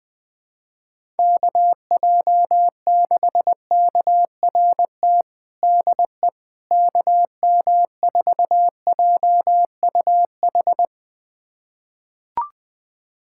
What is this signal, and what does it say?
Morse code, heard as a keyed tone, about 21 wpm, 700 Hz. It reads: KJ6KRT DE KM4JUH
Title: USA Callsign DE Callsign 20wpm